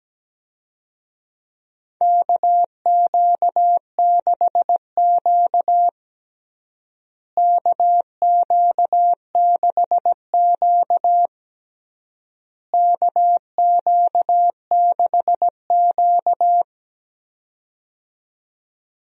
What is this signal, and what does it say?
Morse code, heard as a keyed tone, about 17 wpm, 700 Hz. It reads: KQ6Q KQ6Q KQ6Q
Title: USA Call Signs - 4 Characters - Repeated 3x - 17wpm